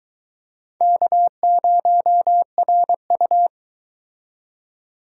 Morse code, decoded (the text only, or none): K0RU